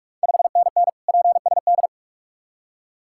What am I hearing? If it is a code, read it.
Morse code, heard as a keyed tone, about 44 wpm, 690 Hz. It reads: 5NN PSD
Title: Ontario QSO Party Exchange 45wpm